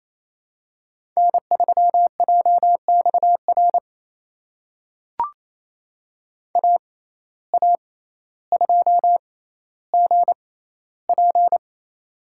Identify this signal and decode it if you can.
Morse code, heard as a keyed tone, about 28 wpm, spaced as for 14 wpm, 700 Hz. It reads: N3JXR AA2GP
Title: USA Call Signs - 5 Characters - 6x Character Spacing - 28wpm